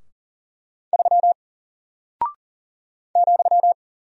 Morse code, heard as a keyed tone, about 40 wpm, 700 Hz. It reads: 3 ,
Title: Single Character (Letter:Number:Punctuation:Prosign) Alt 40wpm